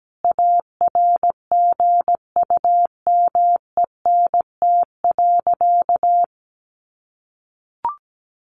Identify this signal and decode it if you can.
Morse code, heard as a keyed tone, about 17 wpm, 700 Hz. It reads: ARGUMENT.
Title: Sentences from Top 1400 Words 17wpm